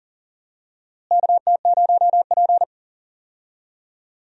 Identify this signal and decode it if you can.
Morse code, heard as a keyed tone, about 40 wpm, 700 Hz. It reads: KT0P
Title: USA Call Signs - 4 Characters - Repeated 3x - 40wpm